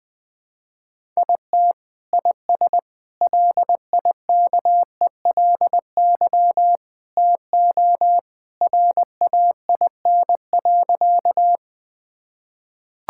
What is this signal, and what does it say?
Morse code, heard as a keyed tone, about 20 wpm, 700 Hz. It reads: IT IS LIKELY TO RAIN.
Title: Sentences with 2-Word Phrases - 20wpm